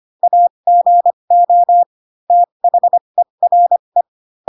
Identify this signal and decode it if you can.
Morse code, heard as a keyed tone, about 25 wpm, spaced as for 22 wpm, 700 Hz. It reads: AGO THERE